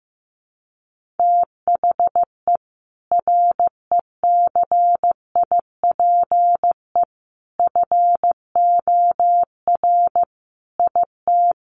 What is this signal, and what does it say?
Morse code, heard as a keyed tone, about 15 wpm, 700 Hz. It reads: THE RECIPE FOR IT
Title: Sentences from Top 1200 Words 15wpm